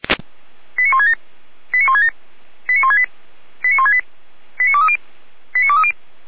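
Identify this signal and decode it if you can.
Five-tone selective call (ZVEI), heard as quick runs of five tones, tones of about 70 ms. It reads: 89167 89167 89168 89168 89230 89230
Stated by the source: Five-tone ZVEI selective call 89166 received by SDR.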